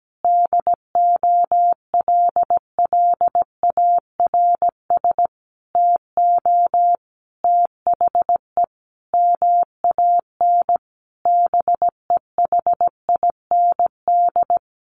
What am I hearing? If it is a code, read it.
Morse code, heard as a keyed tone, about 17 wpm, 700 Hz. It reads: DOLLARS TO THE MAN BEHIND
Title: A Sound Of Thunder 17wpm